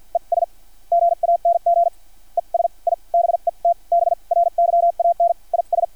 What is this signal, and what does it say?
Morse code, heard as a keyed tone, about 50 wpm, spaced as for 40 wpm, 690 Hz. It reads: ES GANG ES I BET BRYAN IS
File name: archived / CRSnet